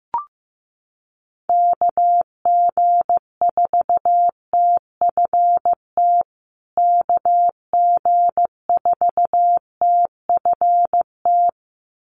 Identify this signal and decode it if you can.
Morse code, heard as a keyed tone, about 15 wpm, 700 Hz. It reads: KG4TFT KG4TFT
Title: USA Call Signs Repeated 2x 15wpm